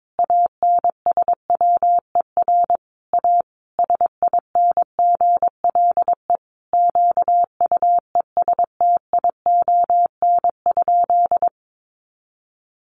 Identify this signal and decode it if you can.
Morse code, heard as a keyed tone, about 22 wpm, 700 Hz. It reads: ANSWER A SINGLE QUESTION?
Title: Sentences from Top 700 Words 22wpm